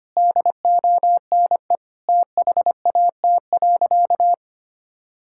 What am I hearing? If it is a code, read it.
Morse code, heard as a keyed tone, about 25 wpm, 700 Hz. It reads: DONE THAT.
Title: Sentences from Top 1600 Words 25wpm